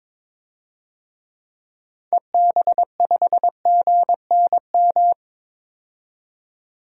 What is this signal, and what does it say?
Morse code, heard as a keyed tone, about 22 wpm, 700 Hz. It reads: EB5GNM
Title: DXCC-100 Call Signs - 6 Characters - 22wpm